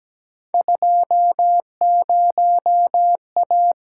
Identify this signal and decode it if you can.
Morse code, heard as a keyed tone, about 17 wpm, 700 Hz. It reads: 20A
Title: Field Day Exchanges Encore - 17wpm